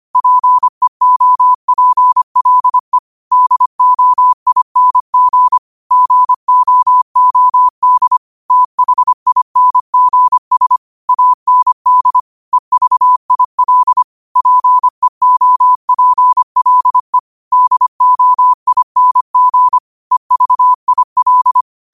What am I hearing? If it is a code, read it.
Morse code, heard as a keyed tone, about 25 wpm, 1000 Hz. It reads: PEOPLE DOING GOOD THINGS AND EVIL PEOPLE DOING EVIL
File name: qotdfiles / 25wpm